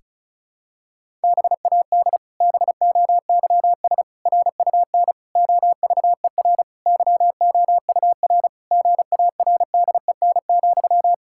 Morse code, heard as a keyed tone, about 36 wpm, 700 Hz. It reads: BAD BOYS RUN OVER YOUR GARDEN,
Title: Pangram Challenge v2 - 35wpm